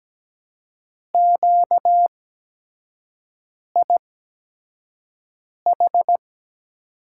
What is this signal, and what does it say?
Morse code, heard as a keyed tone, about 17 wpm, spaced as for 5 wpm, 700 Hz. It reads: QIH
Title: USA Call Signs - 8x Character Spacing - 17wpm